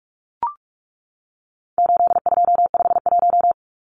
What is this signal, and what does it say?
Morse code, heard as a keyed tone, about 44 wpm, 700 Hz. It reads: 8251
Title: Numbers - 4-Digit 45wpm